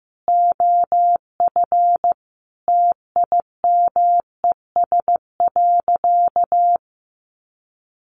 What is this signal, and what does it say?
Morse code, heard as a keyed tone, about 15 wpm, 700 Hz. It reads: OF TIMES.
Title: Sentences from Top 1100 Words 15wpm